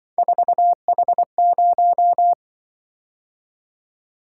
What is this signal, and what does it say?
Morse code, heard as a keyed tone, about 25 wpm, 700 Hz. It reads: 4H0
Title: Speed-Racing - Three Letter-Number Combinations - 20wpm